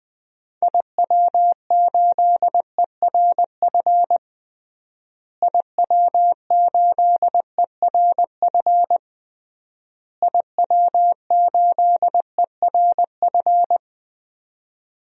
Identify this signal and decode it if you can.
Morse code, heard as a keyed tone, about 20 wpm, 700 Hz. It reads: IW8ERF IW8ERF IW8ERF
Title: DXCC-100 Call Signs - 6 Characters - Repeated 3x - 20wpm